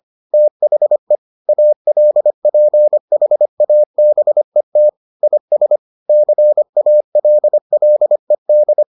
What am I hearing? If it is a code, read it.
Morse code, heard as a keyed tone, about 25 wpm, 600 Hz. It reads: THE ALPHABET IS CALLED